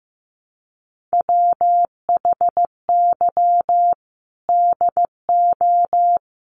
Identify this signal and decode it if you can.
Morse code, heard as a keyed tone, about 15 wpm, 700 Hz. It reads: WHY DO
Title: Sentences from Top 1400 Words 15wpm